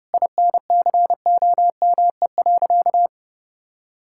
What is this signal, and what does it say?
Morse code, heard as a keyed tone, about 30 wpm, 700 Hz. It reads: INCOME.
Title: Sentences from Top 1000 Words - Review Part 5 - 30wpm